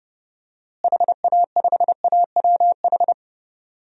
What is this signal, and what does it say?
Morse code, heard as a keyed tone, about 30 wpm, 700 Hz. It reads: HA5AWH
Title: DXCC-100 Call Signs - 6 Characters - Repeated 3x - 30wpm